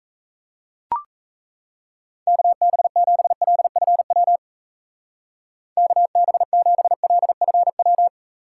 Morse code, heard as a keyed tone, about 34 wpm, 700 Hz. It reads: KB7LFW KB7LFW
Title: Speed Racing - USA 2x3 Call Signs 28wpm